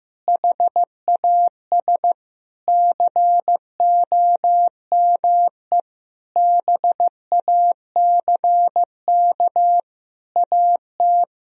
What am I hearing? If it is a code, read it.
Morse code, heard as a keyed tone, about 15 wpm, 700 Hz. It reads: HAS COME BACK AT
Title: Sentences from Top 1000 Words - Review Part 2 - 15wpm